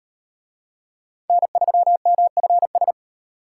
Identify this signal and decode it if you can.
Morse code, heard as a keyed tone, about 38 wpm, 700 Hz. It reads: N3MFS